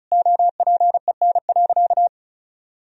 Morse code, heard as a keyed tone, about 36 wpm, 700 Hz. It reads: OPEN.
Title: Sentences from Top 1000 Words - Review Part 1 - 35wpm